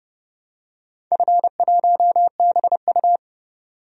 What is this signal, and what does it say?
Morse code, heard as a keyed tone, about 32 wpm, 700 Hz. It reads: F1BU
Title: DXCC-100 Call Signs - 4 Characters - 30wpm